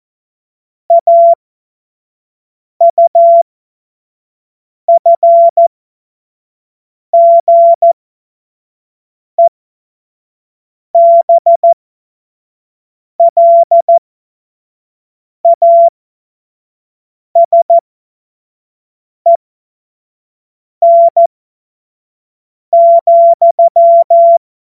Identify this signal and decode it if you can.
Morse code, heard as a keyed tone, about 14 wpm, spaced as for 5 wpm, 680 Hz. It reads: AUFGEBLASEN,
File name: Morsekurs / Witze / Witz 19. April 2022 25 F70